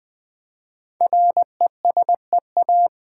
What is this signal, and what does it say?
Morse code, heard as a keyed tone, about 20 wpm, 700 Hz. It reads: RESEA
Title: Build A Word - 5 Letters At A Time - 10-Letter Words - 20wpm